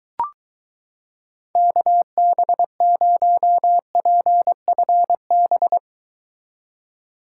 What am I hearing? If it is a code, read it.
Morse code, heard as a keyed tone, about 23 wpm, 700 Hz. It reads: KB0PFB